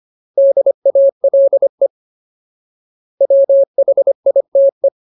Morse code, heard as a keyed tone, about 25 wpm, 550 Hz. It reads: DALE WHITE
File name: Ultimate Word List / 25WPM Ultimate